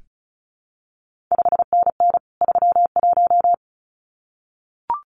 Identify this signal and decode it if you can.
Morse code, heard as a keyed tone, about 36 wpm, 700 Hz. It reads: 5NN 31
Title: CQ WW DX Contest Exchange - 35wpm